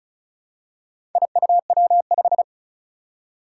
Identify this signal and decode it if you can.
Morse code, heard as a keyed tone, about 35 wpm, 700 Hz. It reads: IUW5
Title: Four Challenging Letter-Number Combinations 35wpm